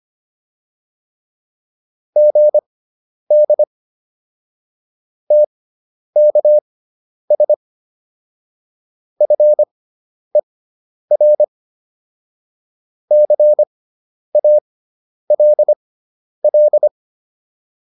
Morse code, heard as a keyed tone, about 25 wpm, spaced as for 10 wpm, 600 Hz. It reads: GD TKS FER CALL